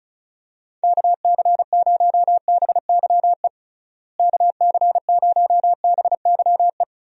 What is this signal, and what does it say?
Morse code, heard as a keyed tone, about 34 wpm, 700 Hz. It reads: KC0BYE KC0BYE
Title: USA Call Signs - 6 Characters - Repeated 2x - 35wpm